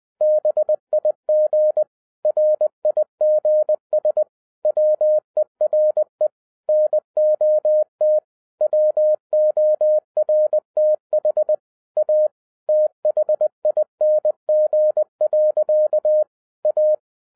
Morse code, heard as a keyed tone, about 20 wpm, 610 Hz. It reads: BIG RIGS WERE NOT WORTH A THING. A
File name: 2011-07-mp3 / n58 BACK FROM A BIG ACCIDENT 20wpm